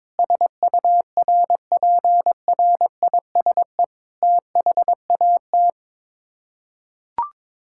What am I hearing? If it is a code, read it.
Morse code, heard as a keyed tone, about 22 wpm, 700 Hz. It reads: SURPRISE THAT
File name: practice / 22wpm / Sets of 7 Words - 22wpm